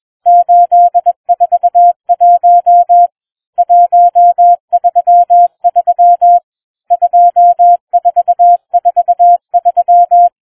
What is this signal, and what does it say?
Morse code, heard as a keyed tone, about 21 wpm, 700 Hz. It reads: 841 133 2443